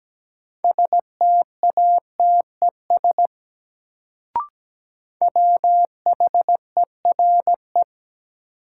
Morse code, heard as a keyed tone, about 17 wpm, 700 Hz. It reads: STATES WHERE
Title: Sets of 2 Words - New Word Drill - 17wpm